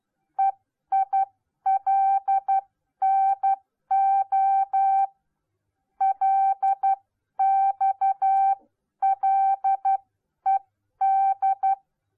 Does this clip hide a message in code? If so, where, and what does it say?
Yes: Morse code from 0.4 to 11.7 s, heard as a keyed tone, about 12 wpm, spaced as for 10 wpm, 780 Hz. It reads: EILNO LXLED